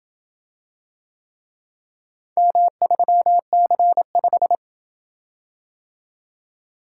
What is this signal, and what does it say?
Morse code, heard as a keyed tone, about 28 wpm, 700 Hz. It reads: M3C5